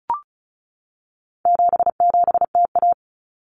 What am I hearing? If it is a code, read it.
Morse code, heard as a keyed tone, about 34 wpm, 700 Hz. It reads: 77TA
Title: Common Cut Numbers - 4-Digit 35wpm